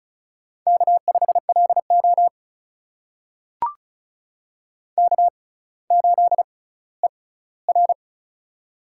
Morse code, heard as a keyed tone, about 36 wpm, spaced as for 18 wpm, 700 Hz. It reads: K5LO K8ER
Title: USA Call Signs - 4 Characters - 6x Character Spacing - 35wpm